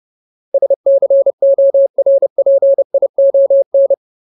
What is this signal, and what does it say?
Morse code, heard as a keyed tone, about 30 wpm, 550 Hz. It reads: SCORPION
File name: Ultimate Word List / 30WPM Ultimate